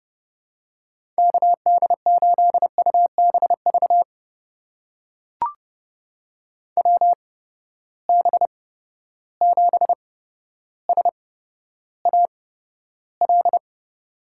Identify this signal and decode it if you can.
Morse code, heard as a keyed tone, about 31 wpm, spaced as for 13 wpm, 700 Hz. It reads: KD8UBV WB7SAL
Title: USA Call Signs - 6 Characters - 8x Character Spacing - 30wpm